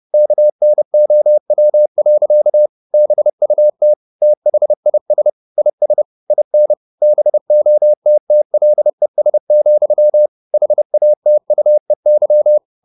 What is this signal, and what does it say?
Morse code, heard as a keyed tone, about 30 wpm, 600 Hz. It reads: KNOW. BUT THIS IS IN BOTTLES, HATUEY
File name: the old man and the sea / the old man and the sea 1 30wpm